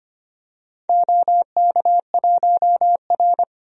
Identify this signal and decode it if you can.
Morse code, heard as a keyed tone, about 25 wpm, 700 Hz. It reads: OK1R
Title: DXCC-100 Call Signs - 4 Characters - Repeated 2x - 25wpm